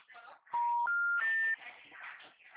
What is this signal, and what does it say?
Special information tone.
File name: out- -2025- - - .21542